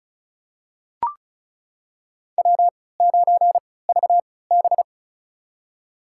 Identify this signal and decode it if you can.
Morse code, heard as a keyed tone, about 36 wpm, spaced as for 20 wpm, 700 Hz. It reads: W9VB
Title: USA Call Signs - 4 Characters - 3x Character Spacing - 35wpm